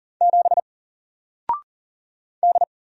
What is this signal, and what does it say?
Morse code, heard as a keyed tone, about 40 wpm, 700 Hz. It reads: 7 D
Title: Single Letter-Number 40wpm